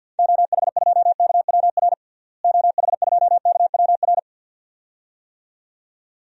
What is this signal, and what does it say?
Morse code, heard as a keyed tone, about 46 wpm, 700 Hz. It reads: OH2KWR OH2KWR
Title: DXCC-100 Call Signs - 6 Characters - Repeated 2x - 50wpm